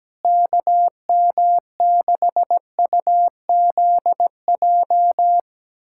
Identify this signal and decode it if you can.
Morse code, heard as a keyed tone, about 17 wpm, 700 Hz. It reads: KM6UZJ
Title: USA Callsign DE Callsign - 6 Characters - 17wpm